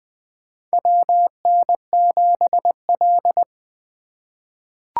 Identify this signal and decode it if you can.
Morse code, heard as a keyed tone, about 20 wpm, 700 Hz. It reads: WN7L
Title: USA Call Signs - 4 Characters - 2x Character Spacing - 20wpm